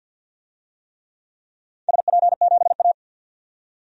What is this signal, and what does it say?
Morse code, heard as a keyed tone, about 47 wpm, 700 Hz. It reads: SP7A
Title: DXCC-100 Call Signs - 4 Characters - 50wpm